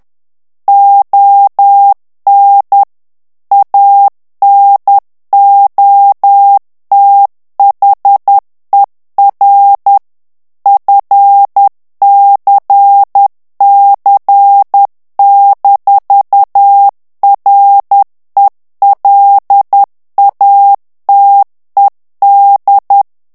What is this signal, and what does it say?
Morse code, heard as a keyed tone, about 10 wpm, 800 Hz. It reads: ON ANOTHER FCC-RELATED
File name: ANGLAIS texte clair 3 vitesse 10 mots minute